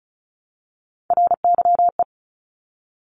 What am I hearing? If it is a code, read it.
Morse code, heard as a keyed tone, about 36 wpm, 700 Hz. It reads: RYE
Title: Multi-letter Phonemes Repeated 3x 35wpm